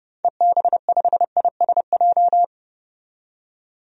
Morse code, heard as a keyed tone, about 30 wpm, 690 Hz. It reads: EB5ISJ
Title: DXCC-100 Call Signs - 6 Characters - 30wpm